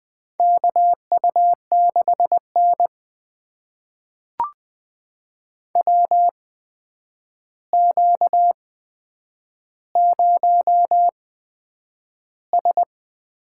Morse code, heard as a keyed tone, about 20 wpm, spaced as for 9 wpm, 700 Hz. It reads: KU6N WQ0S
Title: USA Call Signs - 4 Characters - 8x Character Spacing - 20wpm